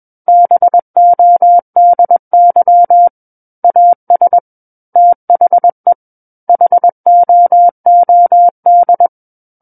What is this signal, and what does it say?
Morse code, heard as a keyed tone, about 21 wpm, 700 Hz. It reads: BODY AS THE HOOD
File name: Book One / 12-Chapter 12 - 21 WPM